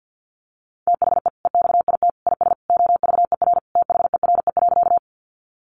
Morse code, heard as a keyed tone, about 50 wpm, 700 Hz. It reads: THE EXIT IS OVER THERE.
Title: Sentences from Top 1100 Words 50wpm